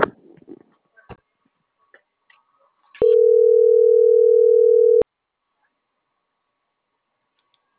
Ringback tone (North America), 440 + 480 Hz.